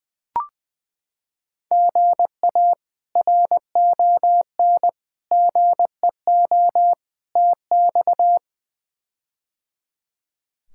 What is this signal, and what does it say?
Morse code, heard as a keyed tone, about 20 wpm, 700 Hz. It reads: GA RON GEO TX
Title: K1USN SST QSO Practice - 20wpm